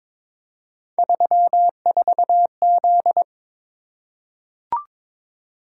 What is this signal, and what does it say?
Morse code, heard as a keyed tone, about 22 wpm, 700 Hz. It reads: 34Z
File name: practice / 22wpm / Three Letter-Number Combinations - 22wpm